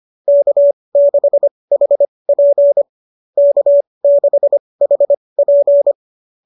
Morse code, heard as a keyed tone, about 25 wpm, spaced as for 20 wpm, 590 Hz. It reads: K6HP K6HP